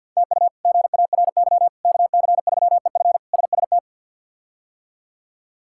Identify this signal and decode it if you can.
Morse code, heard as a keyed tone, about 53 wpm, 700 Hz. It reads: TU GARY KC3EF SST
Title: K1USN SST QSO Practice - 50wpm